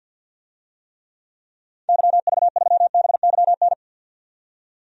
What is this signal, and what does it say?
Morse code, heard as a keyed tone, about 48 wpm, 700 Hz. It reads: YV3BYN
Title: DXCC-100 Call Signs - 6 Characters - 50wpm